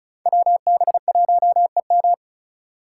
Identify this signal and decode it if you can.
Morse code, heard as a keyed tone, about 36 wpm, 700 Hz. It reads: WB1EM
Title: Code Groups - 5 Letter:Numbers - 35wpm